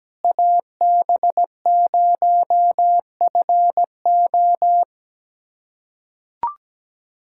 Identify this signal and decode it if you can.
Morse code, heard as a keyed tone, about 17 wpm, 700 Hz. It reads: AB0FO
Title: USA Callsign DE Callsign - 5 Characters - 17wpm